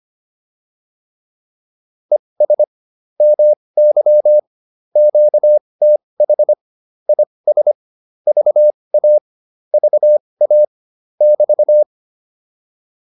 Morse code, heard as a keyed tone, about 25 wpm, spaced as for 20 wpm, 600 Hz. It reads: ES MY QTH IS VA VA <BT>